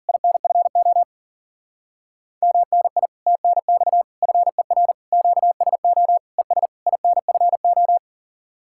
Morse code, heard as a keyed tone, about 50 wpm, 700 Hz. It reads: INFO MNI TNX FER QSO ES INFO
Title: Speed-Racing Common QSO Phrases - 40wpm